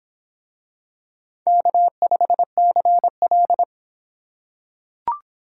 Morse code, heard as a keyed tone, about 26 wpm, 700 Hz. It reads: K5CL